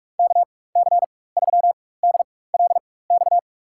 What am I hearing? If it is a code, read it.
Morse code, heard as a keyed tone, about 47 wpm, spaced as for 21 wpm, 700 Hz. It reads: KC3DLX
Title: USA Call Signs - 6 Characters - 4x Character Spacing - 45wpm